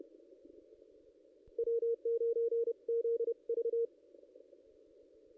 Morse code, heard as a keyed tone, about 31 wpm, 440 Hz. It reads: W9ZV
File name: contests / ss-cw-k5tr-op-n5rz -1600